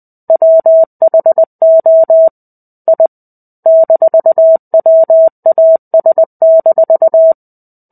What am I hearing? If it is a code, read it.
Morse code, heard as a keyed tone, about 20 wpm, 650 Hz. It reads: WHO I -WAS-
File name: 05-Chapter-05 20 WPM